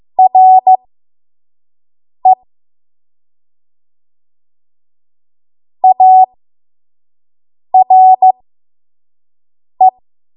Morse code, heard as a keyed tone, about 15 wpm, spaced as for 5 wpm, 750 Hz. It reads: RE ARE
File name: Morse / 05WPM